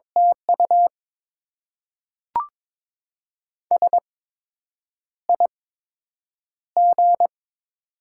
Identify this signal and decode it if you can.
Morse code, heard as a keyed tone, about 22 wpm, spaced as for 8 wpm, 700 Hz. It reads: TU SIG
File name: Top 90 QSO Elements - 8x Character Spacing - 22wpm